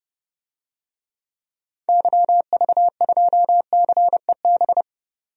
Morse code, heard as a keyed tone, about 30 wpm, 700 Hz. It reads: YV2CEB
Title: DXCC-100 Call Signs - 6 Characters - 30wpm